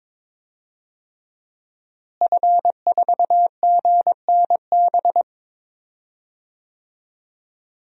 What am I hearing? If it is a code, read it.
Morse code, heard as a keyed tone, about 22 wpm, 700 Hz. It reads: F4GNB
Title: DXCC-100 Call Signs - 5 Characters - Repeated 2x - 22wpm